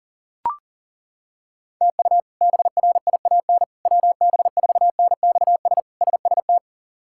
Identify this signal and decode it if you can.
Morse code, heard as a keyed tone, about 40 wpm, 700 Hz. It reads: TU BRIAN WB4NXS SST
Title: K1USN SST QSO Practice - 40wpm